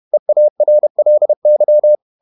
Morse code, heard as a keyed tone, about 31 wpm, 600 Hz. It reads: EARLY
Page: Never 31wpm | CW med Gnister